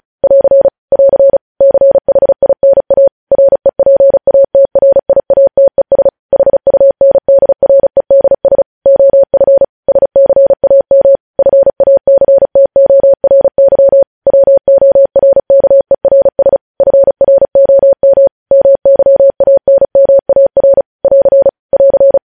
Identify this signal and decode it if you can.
Morse code, heard as a keyed tone, about 36 wpm, 550 Hz. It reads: <AR> <AR> CHINA REPATRIATES HUNDREDS OF SCAM FACTORY WORKERS FROM MYANMAR <AR> <AR>
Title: News Headlines at 35